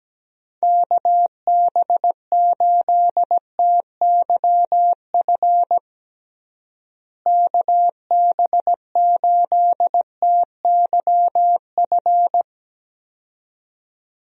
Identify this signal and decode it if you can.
Morse code, heard as a keyed tone, about 17 wpm, 700 Hz. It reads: KB8TYF KB8TYF
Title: USA Call Signs - 6 Characters - Repeated 2x - 17wpm